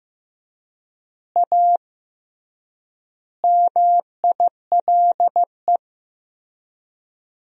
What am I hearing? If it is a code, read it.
Morse code, heard as a keyed tone, about 15 wpm, 700 Hz. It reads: A MILE
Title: 3x Word-Spacing - Sentences from Top 500 Words - 15wpm